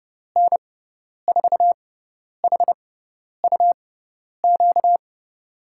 Morse code, heard as a keyed tone, about 30 wpm, spaced as for 10 wpm, 700 Hz. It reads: N4HUQ
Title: USA Call Signs - 6 Characters - 6x Character Spacing - 30wpm